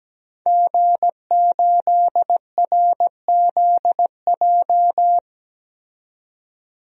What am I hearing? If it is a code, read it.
Morse code, heard as a keyed tone, about 17 wpm, 700 Hz. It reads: G8RZJ
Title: DXCC-100 Call Signs - 5 Characters - 17wpm